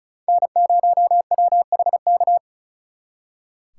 Morse code, heard as a keyed tone, about 44 wpm, 700 Hz. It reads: N0WHK